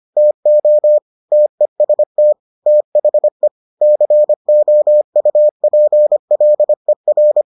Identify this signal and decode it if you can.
Morse code, heard as a keyed tone, about 25 wpm, 600 Hz. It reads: TO TEST THE COUPLER